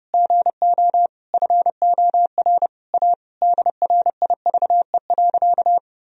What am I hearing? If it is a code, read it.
Morse code, heard as a keyed tone, about 30 wpm, 700 Hz. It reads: GO FOR A DRIVE.
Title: Sentences from Top 1000 Words - Review Part 2 - 30wpm